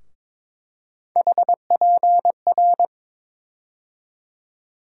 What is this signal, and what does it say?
Morse code, heard as a keyed tone, about 22 wpm, 700 Hz. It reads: HPR
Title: USA Call Sign Postfixes - 3 Letters 22wpm